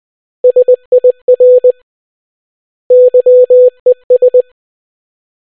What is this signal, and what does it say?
Morse code, heard as a keyed tone, about 20 wpm, 500 Hz. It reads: SIR YES